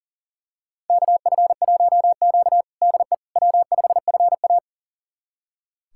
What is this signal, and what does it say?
Morse code, heard as a keyed tone, about 40 wpm, 700 Hz. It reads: KF1Q DE W5FA
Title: USA Callsign DE Callsign - 4 Characters - 40wpm